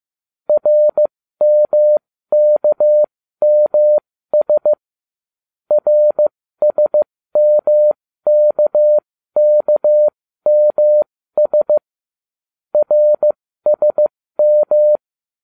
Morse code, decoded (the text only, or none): RMKMS RSMKKMS RSM